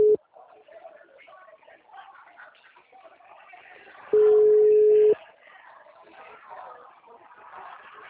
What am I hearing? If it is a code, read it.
Ringback tone (Europe, 425 Hz), with speech.
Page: Limeira 8